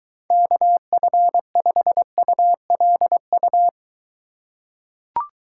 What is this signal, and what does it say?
Morse code, heard as a keyed tone, about 22 wpm, 700 Hz. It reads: KF5ULU